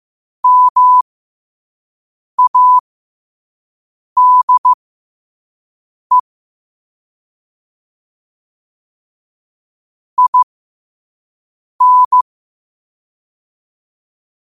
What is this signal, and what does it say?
Morse code, heard as a keyed tone, about 15 wpm, spaced as for 5 wpm, 1000 Hz. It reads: MADE IN
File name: qotdfiles / 5wpm